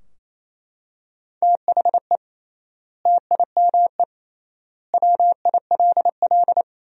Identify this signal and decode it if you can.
Morse code, heard as a keyed tone, about 28 wpm, 700 Hz. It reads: THE TIME WILL
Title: 3x Word-Spacing - Sentences from Top 100 Words - 28wpm